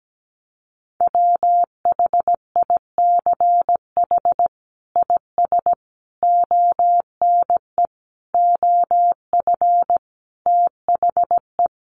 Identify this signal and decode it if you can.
Morse code, heard as a keyed tone, about 17 wpm, 700 Hz. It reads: WHICH IS ONE OF THE
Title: Sets of 5 Words Encore 17wpm